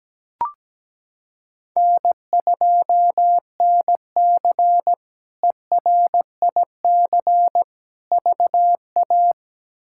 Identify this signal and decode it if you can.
Morse code, heard as a keyed tone, about 17 wpm, 700 Hz. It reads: N2NC ERIC VA
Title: K1USN SST QSO Practice - 17wpm